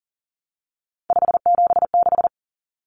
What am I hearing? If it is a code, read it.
Morse code, heard as a keyed tone, about 40 wpm, 700 Hz. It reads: 576
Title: Common Cut Numbers - 3-Digit 40wpm